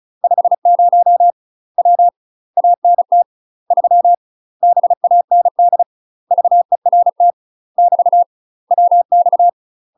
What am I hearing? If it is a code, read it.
Morse code, heard as a keyed tone, about 34 wpm, 700 Hz. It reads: 50 W ANT 3 BAND VERT <BT> WX
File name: UK MASTER QSOs TRAINING / 35WPM / UK MASTER QSOs 0034